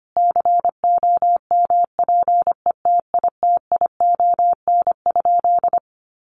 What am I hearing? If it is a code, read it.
Morse code, heard as a keyed tone, about 25 wpm, 700 Hz. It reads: COMPETITION?
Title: Sentences from Top 1000 Words 25wpm